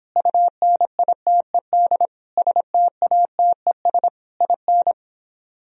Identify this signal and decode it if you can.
Morse code, heard as a keyed tone, about 26 wpm, 700 Hz. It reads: UNITED STATES IN